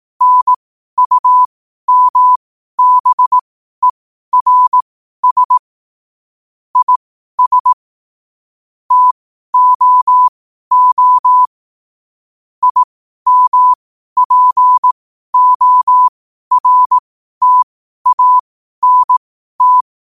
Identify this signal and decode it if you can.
Morse code, heard as a keyed tone, about 18 wpm, spaced as for 12 wpm, 1000 Hz. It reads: NUMBERS IS TOO IMPORTANT